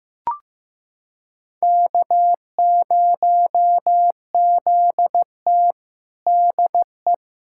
Morse code, heard as a keyed tone, about 16 wpm, 700 Hz. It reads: K0ZT DE
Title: USA Callsign DE Callsign - 4 Characters - 15wpm